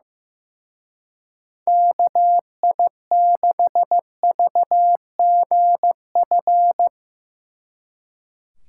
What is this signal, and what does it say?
Morse code, heard as a keyed tone, about 17 wpm, 700 Hz. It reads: KI6VGF
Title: Speed Racing - USA 2x3 Call Signs 15wpm